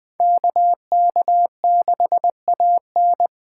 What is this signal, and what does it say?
Morse code, heard as a keyed tone, about 24 wpm, 700 Hz. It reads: KK6AN
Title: Speed Racing - USA 2x2 Call Signs 20wpm